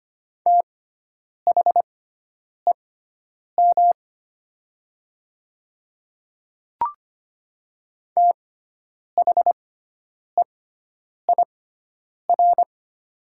Morse code, heard as a keyed tone, about 25 wpm, spaced as for 9 wpm, 700 Hz. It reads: THEM THEIR
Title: 6x Character Spacing - Top 200 Words - 25wpm